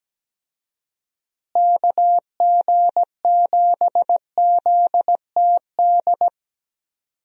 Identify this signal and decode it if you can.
Morse code, heard as a keyed tone, about 17 wpm, 700 Hz. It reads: KG7ZTD
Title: USA Call Signs - 6 Characters - Repeated 3x - 17wpm